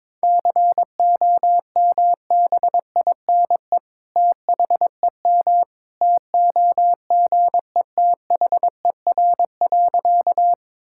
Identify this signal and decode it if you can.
Morse code, heard as a keyed tone, about 21 wpm, 700 Hz. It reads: COMBINE THEM TOGETHER.
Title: Sentences from Top 1600 Words 22wpm